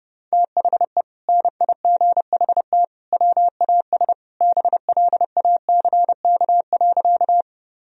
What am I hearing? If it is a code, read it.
Morse code, heard as a keyed tone, about 30 wpm, 700 Hz. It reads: THE NIGHT WAS BLACK.
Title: Sentences from Top 1000 Words - Review Part 2 - 30wpm